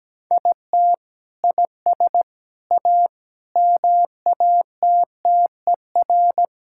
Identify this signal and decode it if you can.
Morse code, heard as a keyed tone, about 17 wpm, 700 Hz. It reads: IT IS A MATTER
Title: Sentences from Top 1000 Words - Review Part 5 - 17wpm